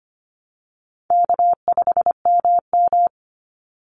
Morse code, heard as a keyed tone, about 25 wpm, 700 Hz. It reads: K5MM
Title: Easy Call Signs Repeated 2x - 25wpm